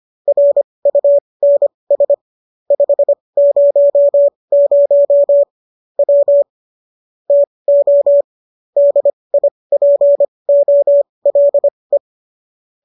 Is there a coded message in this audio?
Yes: Morse code from 0.3 to 12.0 s, heard as a keyed tone, about 25 wpm, spaced as for 20 wpm, 570 Hz. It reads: RUNS 500 W TO DIPOLE